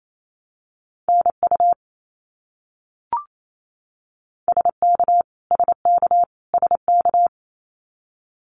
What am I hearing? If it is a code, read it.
Morse code, heard as a keyed tone, about 28 wpm, 700 Hz. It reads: NU SK SK SK
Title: Canadian Provinces and Territories - Repeated 3x - 28wpm